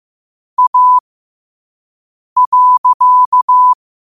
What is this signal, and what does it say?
Morse code, heard as a keyed tone, about 15 wpm, spaced as for 5 wpm, 1000 Hz. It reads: A.
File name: qotdfiles / 5wpm